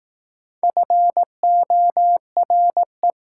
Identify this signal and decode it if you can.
Morse code, heard as a keyed tone, about 18 wpm, 700 Hz. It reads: FORE
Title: Speed-Racing Prefixes and Suffixes 15wpm